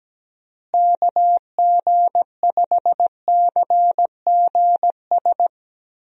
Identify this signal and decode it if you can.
Morse code, heard as a keyed tone, about 17 wpm, 700 Hz. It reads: KG5CGS
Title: USA Call Signs - 6 Characters - Repeated 3x - 17wpm